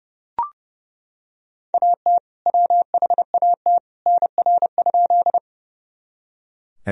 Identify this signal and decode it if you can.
Morse code, heard as a keyed tone, about 30 wpm, 700 Hz. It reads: AT WHAT NR?
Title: Sentences from Top 100 Words and Abbreviations 30wpm